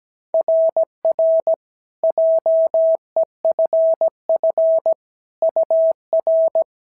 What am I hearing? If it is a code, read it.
Morse code, heard as a keyed tone, about 17 wpm, 690 Hz. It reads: RR JEFF UR
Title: SOTA QSO Practice 17wpm